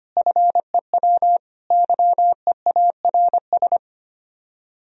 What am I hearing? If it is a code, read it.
Morse code, heard as a keyed tone, about 31 wpm, 700 Hz. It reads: FEW YEARS
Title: Speed-Racing - Sets of 3 Words 25wpm